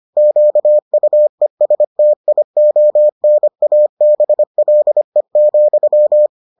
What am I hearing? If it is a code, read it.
Morse code, heard as a keyed tone, about 25 wpm, 600 Hz. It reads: QUESTIONABLE,